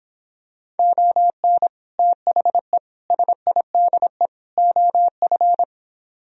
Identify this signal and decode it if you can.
Morse code, heard as a keyed tone, about 26 wpm, 700 Hz. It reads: ON THE SIDE OF